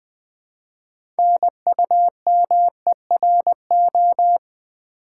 Morse code, heard as a keyed tone, about 20 wpm, 700 Hz. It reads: NUMERO
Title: Build A Word - 2 Letters At A Time - 8-Letter Words - 20wpm